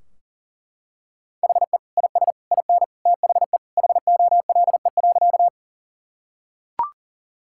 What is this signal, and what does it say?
Morse code, heard as a keyed tone, about 40 wpm, 700 Hz. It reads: HE IS IN THE HOLE.
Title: Sentences from Top 1600 Words 40wpm